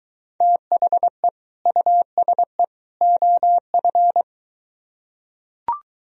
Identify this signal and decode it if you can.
Morse code, heard as a keyed tone, about 21 wpm, 700 Hz. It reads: THE USE OF